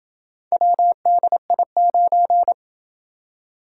Morse code, heard as a keyed tone, about 24 wpm, 700 Hz. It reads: WDI9
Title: Speed-Racing - Four Letter-Number Combinations - 20wpm